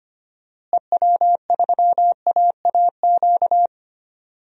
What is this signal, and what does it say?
Morse code, heard as a keyed tone, about 25 wpm, 700 Hz. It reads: EW3AAQ
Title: DXCC-100 Call Signs - 6 Characters - Repeated 2x - 25wpm